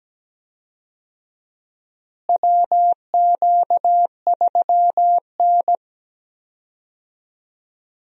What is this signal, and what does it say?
Morse code, heard as a keyed tone, about 17 wpm, 700 Hz. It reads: WQ3N
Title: USA Call Signs - 4 Characters - Repeated 3x - 17wpm